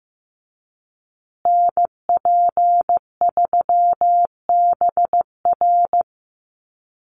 Morse code, heard as a keyed tone, about 15 wpm, 700 Hz. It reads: NP3BR
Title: USA Call Signs Repeated 2x 15wpm